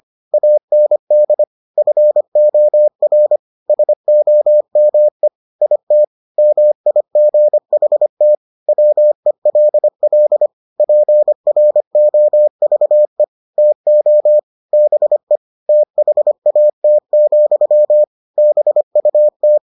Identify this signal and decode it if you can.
Morse code, heard as a keyed tone, about 25 wpm, 600 Hz. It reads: AND FOR SOME IT MIGHT WELL PROVE TO BE THAT, BUT